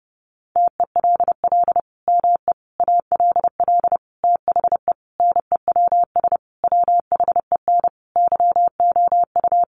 Morse code, heard as a keyed tone, about 30 wpm, 700 Hz. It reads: TELL ME ALL THE NEWS WHEN YOU
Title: Sentences from Top 800 Words 30wpm